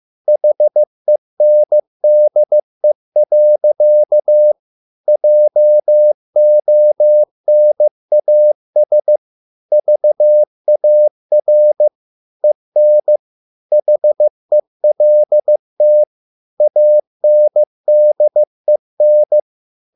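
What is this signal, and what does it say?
Morse code, heard as a keyed tone, about 15 wpm, 600 Hz. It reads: HENDE. JONAS VAR EN HELT ANDEN
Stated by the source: Morse code at a slow 15 wpm from CW med Gnister.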